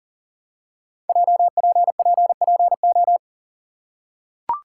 Morse code, heard as a keyed tone, about 42 wpm, 690 Hz. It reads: JPPPO
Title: Code Groups - 5 Letters - 40wpm